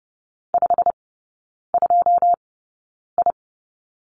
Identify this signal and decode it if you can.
Morse code, heard as a keyed tone, about 30 wpm, spaced as for 9 wpm, 700 Hz. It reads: 52I
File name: Lesson 19e - Letter w - Three Characters 30wpm